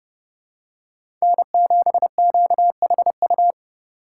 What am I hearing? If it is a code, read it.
Morse code, heard as a keyed tone, about 31 wpm, 700 Hz. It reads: N7QHU